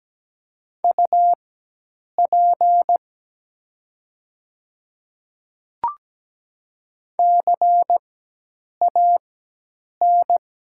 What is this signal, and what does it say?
Morse code, heard as a keyed tone, about 17 wpm, spaced as for 8 wpm, 700 Hz. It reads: UP CAN
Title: 4x Character Spacing - Top 100 Words - 17wpm